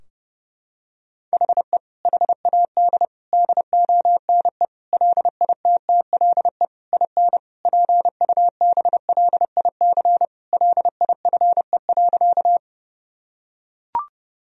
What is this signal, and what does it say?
Morse code, heard as a keyed tone, about 30 wpm, 700 Hz. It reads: HE HAD DONE LITTLE IN PUBLIC LIFE.
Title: Sentences from Top 1000 Words - Review Part 3 - 30wpm